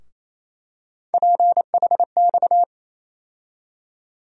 Morse code, heard as a keyed tone, about 28 wpm, 700 Hz. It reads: PHX
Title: USA Call Sign Postfixes - 3 Letters 28wpm